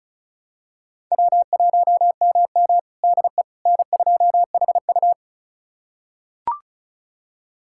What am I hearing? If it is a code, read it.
Morse code, heard as a keyed tone, about 36 wpm, 700 Hz. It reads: W1MM DE N2HU
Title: USA Callsign DE Callsign - 4 Characters - 35wpm